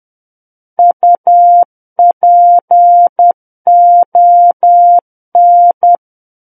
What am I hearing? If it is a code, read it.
Morse code, heard as a keyed tone, about 10 wpm, 700 Hz. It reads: UPON
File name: Book One / 01-Chapter 1 - 10 WPM